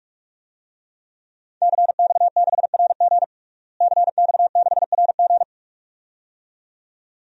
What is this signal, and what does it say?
Morse code, heard as a keyed tone, about 46 wpm, 700 Hz. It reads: CX6RG CX6RG
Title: DXCC-100 Call Signs - 5 Characters - Repeated 2x - 45wpm